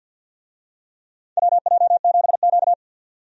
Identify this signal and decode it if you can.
Morse code, heard as a keyed tone, about 46 wpm, 700 Hz. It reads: WJ7Q
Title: USA Call Signs - 4 Characters - 50wpm